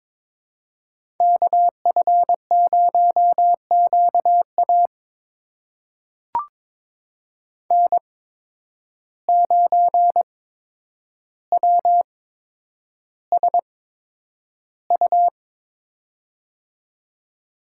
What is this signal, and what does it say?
Morse code, heard as a keyed tone, about 22 wpm, spaced as for 9 wpm, 700 Hz. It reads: KF0QA N9WSU
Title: USA Call Signs - 5 Characters - 8x Character Spacing - 22wpm